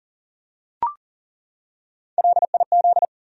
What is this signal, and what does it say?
Morse code, heard as a keyed tone, about 40 wpm, 700 Hz. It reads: LIZ
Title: USA Common Names - Max 3 Letters - 40wpm